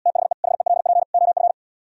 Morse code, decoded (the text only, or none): THE HELP OF